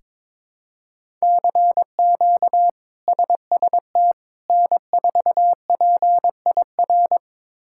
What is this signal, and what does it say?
Morse code, heard as a keyed tone, about 22 wpm, 700 Hz. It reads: CQ SST N4PIR
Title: K1USN SST QSO Practice - 22wpm